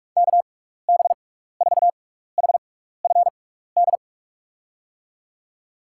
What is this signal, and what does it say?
Morse code, heard as a keyed tone, about 43 wpm, spaced as for 15 wpm, 700 Hz. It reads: KB4HFD
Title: USA Call Signs - 6 Characters - 6x Character Spacing - 45wpm